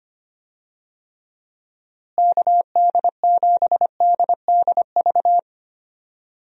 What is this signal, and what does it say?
Morse code, heard as a keyed tone, about 25 wpm, 700 Hz. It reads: KD7DDV
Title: USA Call Signs - 6 Characters - Repeated 2x - 25wpm